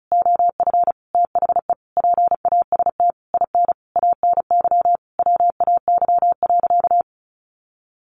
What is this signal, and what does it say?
Morse code, heard as a keyed tone, about 34 wpm, 700 Hz. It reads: OF THE PAST IN ANY WAY.
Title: A Sound Of Thunder 35wpm